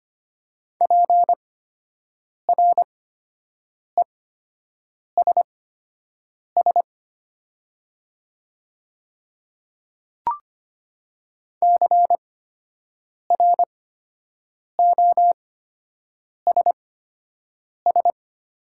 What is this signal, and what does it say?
Morse code, heard as a keyed tone, about 25 wpm, spaced as for 7 wpm, 700 Hz. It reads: PRESS CROSS
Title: 8x Character Spacing - Top 500 Words - 25wpm